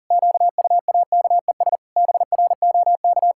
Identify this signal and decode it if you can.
Morse code, heard as a keyed tone, about 40 wpm, 700 Hz. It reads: QUAKES BROK
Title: Pangram Challenge v2 - 40wpm